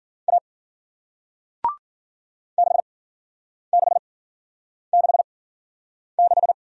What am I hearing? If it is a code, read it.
Morse code, heard as a keyed tone, about 50 wpm, spaced as for 9 wpm, 700 Hz. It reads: A 6666
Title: Speed-Racing Single Letter-Number 40wpm